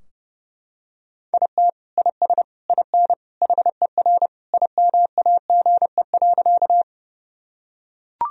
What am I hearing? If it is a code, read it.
Morse code, heard as a keyed tone, about 30 wpm, 700 Hz. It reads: IT IS IN HER IMAGE.
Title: Sentences from Top 1000 Words - Review Part 4 - 30wpm